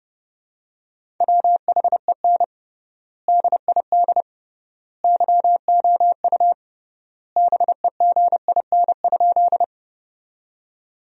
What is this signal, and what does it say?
Morse code, heard as a keyed tone, about 31 wpm, 700 Hz. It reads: WHEN DID YOU BEGIN?
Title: 3x Word-Spacing - Sentences from Top 300 Words - 30wpm